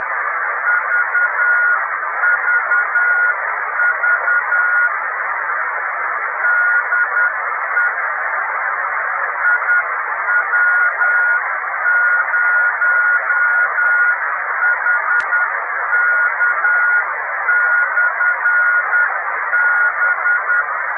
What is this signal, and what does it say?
Morse code, heard as a keyed tone, about 10 wpm, 1440 Hz, with radio noise. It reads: VVV DE IW0HK/B
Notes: Beacon amatoriale italiano 28322 KHz